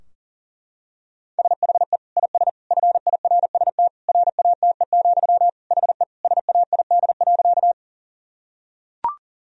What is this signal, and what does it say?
Morse code, heard as a keyed tone, about 37 wpm, 700 Hz. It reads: SHE IS FIRST RATE, HE SAID.
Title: Sentences from Top 1000 Words - Review Part 4 - 40wpm